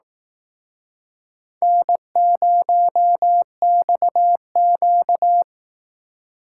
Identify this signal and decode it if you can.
Morse code, heard as a keyed tone, about 19 wpm, 700 Hz. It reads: N0XQ
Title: Speed Racing - USA Call Signs 15wpm